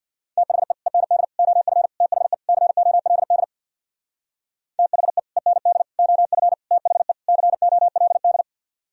Morse code, heard as a keyed tone, about 64 wpm, 700 Hz. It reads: THE END OF THE COLD THE END OF THE COLD